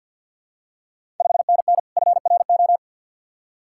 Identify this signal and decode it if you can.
Morse code, heard as a keyed tone, about 51 wpm, 690 Hz. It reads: 5NN FRO
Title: Ontario QSO Party Exchange 50wpm